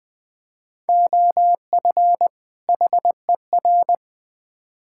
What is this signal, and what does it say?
Morse code, heard as a keyed tone, about 20 wpm, 700 Hz. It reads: OF HER
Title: Build A Phrase - 2-Word Phrases - 20wpm